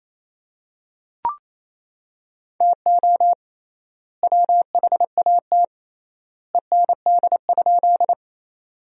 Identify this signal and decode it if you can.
Morse code, heard as a keyed tone, about 28 wpm, 700 Hz. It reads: TO WHAT END?
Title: 3x Word-Spacing - Sentences from Top 200 Words - 28wpm